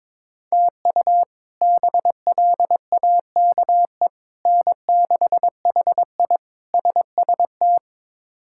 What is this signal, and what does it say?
Morse code, heard as a keyed tone, about 22 wpm, 700 Hz. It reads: TU BLAKE N6HI SST
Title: K1USN SST QSO Practice - 22wpm